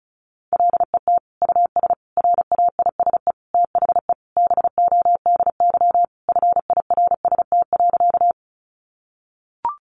Morse code, heard as a keyed tone, about 34 wpm, 700 Hz. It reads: LET US RAISE THE BODY FIRST.
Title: Sentences from Top 1400 Words 35wpm